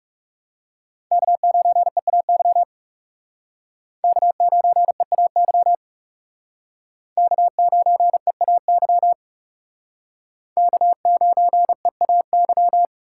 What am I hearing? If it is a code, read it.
Morse code, heard as a keyed tone, about 37 wpm, 700 Hz. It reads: K9EAY K9EAY K9EAY K9EAY
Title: Speed Racing - USA 1x3 Call Signs 30wpm